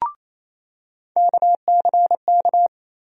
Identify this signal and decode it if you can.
Morse code, heard as a keyed tone, about 28 wpm, 700 Hz. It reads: KCK
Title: USA Call Sign Postfixes - 3 Letters 28wpm